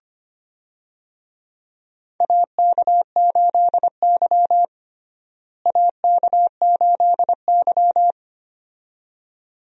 Morse code, heard as a keyed tone, about 25 wpm, 700 Hz. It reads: AK8Y AK8Y
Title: USA Call Signs - 4 Characters - Repeated 2x - 25wpm